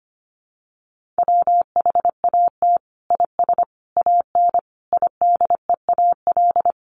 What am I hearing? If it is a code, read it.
Morse code, heard as a keyed tone, about 25 wpm, 700 Hz. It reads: WHAT IS AN IDEAL
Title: Sentences from Top 1400 Words 25wpm